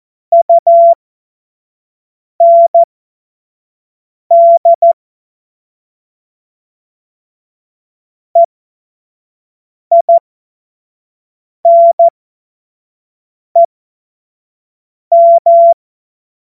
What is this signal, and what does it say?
Morse code, heard as a keyed tone, about 14 wpm, spaced as for 5 wpm, 680 Hz. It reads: UND EINEM
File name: Witz-14-Oktober-2025-25-F70